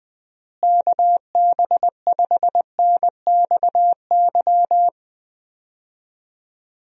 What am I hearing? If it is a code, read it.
Morse code, heard as a keyed tone, about 19 wpm, 700 Hz. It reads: KB5NXY
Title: USA Call Signs - 6 Characters - Repeated 3x - 20wpm